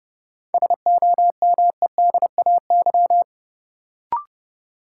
Morse code, heard as a keyed tone, about 32 wpm, 700 Hz. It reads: SOMEDAY
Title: Sentences from Top 1600 Words - New Word Drill - 30wpm